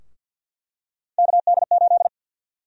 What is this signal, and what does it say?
Morse code, heard as a keyed tone, about 51 wpm, 700 Hz. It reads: KD8
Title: USA Call Sign Prefixes - 50wpm